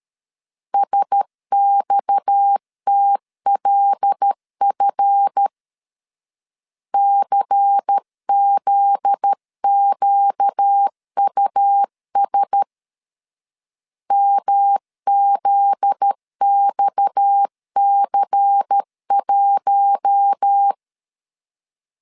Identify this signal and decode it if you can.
Morse code, heard as a keyed tone, about 13 wpm, 800 Hz. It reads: SXTLF CZQUS MZXC1
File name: LEZIONI / 7 GF9Z / 7 GF9Z 12wpm 2147